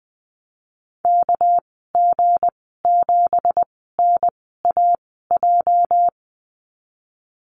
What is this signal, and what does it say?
Morse code, heard as a keyed tone, about 20 wpm, spaced as for 15 wpm, 700 Hz. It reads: KG7NAJ
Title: 213 Lesson 34g - Number 8 - Call Signs 20wpm